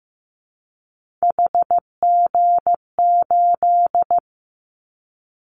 Morse code, heard as a keyed tone, about 15 wpm, 700 Hz. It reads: HG8
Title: Callsign Prefixes for Top 50 DXCC - 15wpm